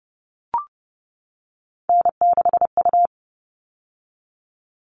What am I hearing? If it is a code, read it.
Morse code, heard as a keyed tone, about 30 wpm, 700 Hz. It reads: N6U
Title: All Cut Numbers - 3-Digit 30wpm